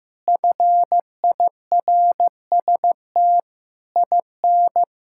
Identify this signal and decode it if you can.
Morse code, heard as a keyed tone, about 15 wpm, 700 Hz. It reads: FIRST IN
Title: Sentences from Top 1000 Words - Review Part 6 - 15wpm